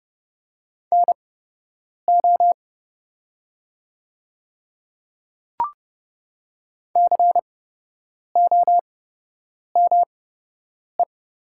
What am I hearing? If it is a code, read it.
Morse code, heard as a keyed tone, about 30 wpm, spaced as for 8 wpm, 700 Hz. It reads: NO COME
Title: 8x Character Spacing - Top 300 Words - 30wpm